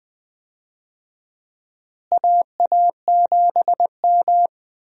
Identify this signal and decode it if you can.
Morse code, heard as a keyed tone, about 20 wpm, 700 Hz. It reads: AA7M
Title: USA Call Signs - 4 Characters - Repeated 3x - 20wpm